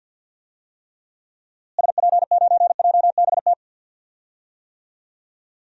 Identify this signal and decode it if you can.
Morse code, heard as a keyed tone, about 47 wpm, 700 Hz. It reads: SP9JBT
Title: DXCC-100 Call Signs - 6 Characters - Repeated 3x - 50wpm